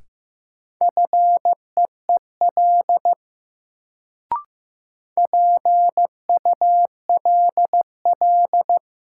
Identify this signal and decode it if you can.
Morse code, heard as a keyed tone, about 15 wpm, 700 Hz. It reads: FEEL PULL
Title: Sentences from Top 400 Words - New Word Drill - 15wpm